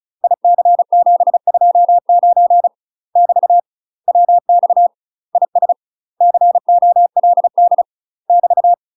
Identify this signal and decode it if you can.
Morse code, heard as a keyed tone, about 34 wpm, 700 Hz. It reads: IC729 <BT> WX IS COLD <BT>
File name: UK MASTER QSOs TRAINING / 35WPM / UK MASTER QSOs 0001